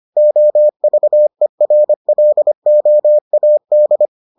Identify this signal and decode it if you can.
Morse code, heard as a keyed tone, about 25 wpm, 600 Hz. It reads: OVERLOAD